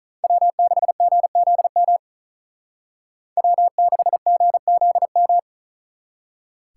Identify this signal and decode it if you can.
Morse code, heard as a keyed tone, about 39 wpm, 700 Hz. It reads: W6GZM W6GZM
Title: Speed Racing - USA 1x3 Call Signs 35wpm